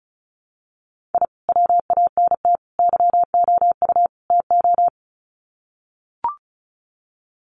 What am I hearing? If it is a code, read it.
Morse code, heard as a keyed tone, about 34 wpm, 700 Hz. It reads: I WANT YOU TO
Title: Sets of 4 Words 35wpm